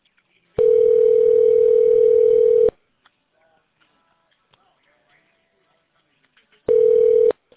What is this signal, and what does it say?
Ringback tone (North America), 440 + 480 Hz.